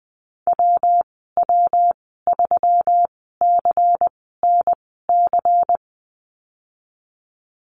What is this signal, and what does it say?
Morse code, heard as a keyed tone, about 20 wpm, spaced as for 15 wpm, 700 Hz. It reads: WW3CNC
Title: 161 Lesson 26g - Letter g - Call Signs 20wpm